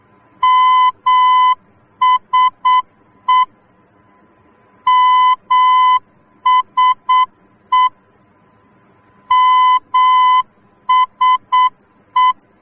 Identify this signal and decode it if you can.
Morse code, heard as a keyed tone, about 8 wpm, 1010 Hz. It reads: MSE MSE MSE